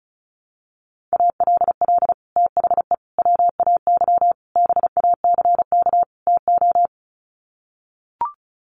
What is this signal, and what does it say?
Morse code, heard as a keyed tone, about 34 wpm, 700 Hz. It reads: ALL THE WAY BACK TO
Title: Sets of 5 Words Encore 35wpm